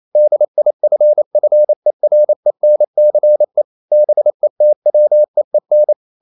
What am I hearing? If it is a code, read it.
Morse code, heard as a keyed tone, about 28 wpm, 600 Hz. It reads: DIFFERENCE BETWEEN
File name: CW / Practice / EnglishQuotations0128